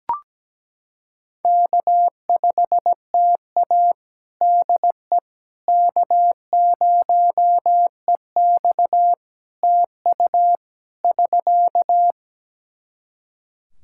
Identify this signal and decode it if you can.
Morse code, heard as a keyed tone, about 17 wpm, 700 Hz. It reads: K5TA DE K0EX TU <SK>
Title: Practice QSO - Exchange 4 - Ending - 17wpm